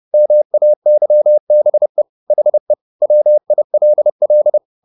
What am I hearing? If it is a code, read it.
Morse code, heard as a keyed tone, about 30 wpm, 600 Hz. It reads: MAYBE HE WILL
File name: the old man and the sea / the old man and the sea 1 30wpm